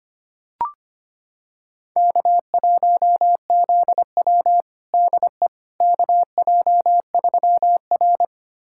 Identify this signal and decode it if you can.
Morse code, heard as a keyed tone, about 25 wpm, 700 Hz. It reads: K1ZW DE KJ3R
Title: USA Callsign DE Callsign - 4 Characters - 25wpm